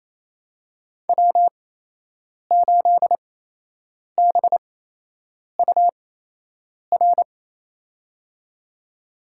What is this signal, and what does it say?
Morse code, heard as a keyed tone, about 28 wpm, spaced as for 8 wpm, 700 Hz. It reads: W8BUR
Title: USA Call Signs - 5 Characters - 8x Character Spacing - 28wpm